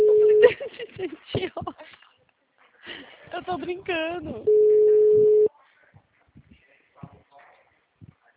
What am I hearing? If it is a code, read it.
Ringback tone (Europe, 425 Hz), with speech.